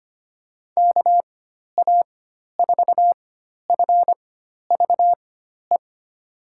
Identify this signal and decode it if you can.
Morse code, heard as a keyed tone, about 25 wpm, spaced as for 12 wpm, 700 Hz. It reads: KA4FVE
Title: USA Call Signs - 6 Characters - 4x Character Spacing - 25wpm